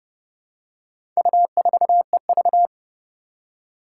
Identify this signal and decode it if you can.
Morse code, heard as a keyed tone, about 32 wpm, 700 Hz. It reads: U4EV
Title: Four Challenging Letter-Number Combinations 30wpm